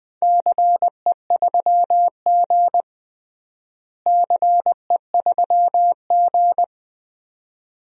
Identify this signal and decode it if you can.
Morse code, heard as a keyed tone, about 20 wpm, 700 Hz. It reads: CE3G CE3G
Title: DXCC-100 Call Signs - 4 Characters - Repeated 2x - 20wpm